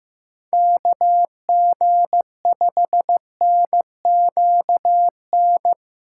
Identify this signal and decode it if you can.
Morse code, heard as a keyed tone, about 15 wpm, 700 Hz. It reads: KG5NQN
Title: USA Callsign DE Callsign 15wpm